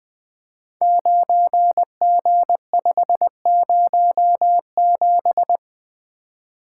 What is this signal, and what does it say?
Morse code, heard as a keyed tone, about 20 wpm, 700 Hz. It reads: 9G507
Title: Code Groups - 5 Letter:Numbers - 20wpm